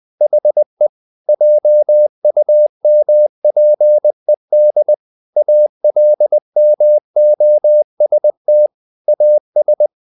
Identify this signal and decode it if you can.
Morse code, heard as a keyed tone, about 20 wpm, 600 Hz. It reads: HE JUMPED ALMOST AS
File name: the old man and the sea / the old man and the sea part 1 20wpm